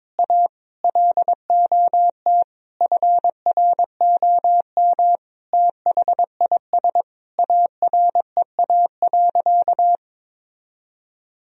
Morse code, heard as a keyed tone, about 22 wpm, 700 Hz. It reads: A LOT FROM THIS AREA.
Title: Sentences from Top 1000 Words - Review Part 2 - 22wpm